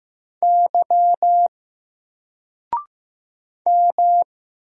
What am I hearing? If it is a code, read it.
Morse code, heard as a keyed tone, about 15 wpm, 700 Hz. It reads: Y M
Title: Single Character (Letter:Number:Punctuation:Prosign) 15wpm